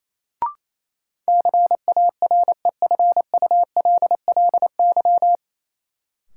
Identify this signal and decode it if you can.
Morse code, heard as a keyed tone, about 28 wpm, 700 Hz. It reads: CAREFULLY
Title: Sentences from Top 1300 Words - New Word Drill Alt - 28wpm